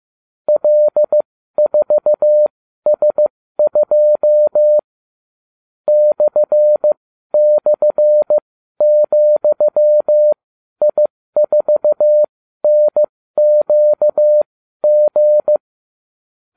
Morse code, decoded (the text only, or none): L4S2 //,I4NQG